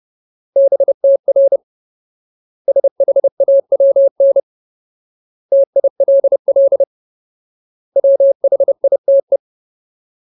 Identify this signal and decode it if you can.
Morse code, heard as a keyed tone, about 30 wpm, 550 Hz. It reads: BTR SHAWN TILL WHITE
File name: Ultimate Word List / 30WPM Ultimate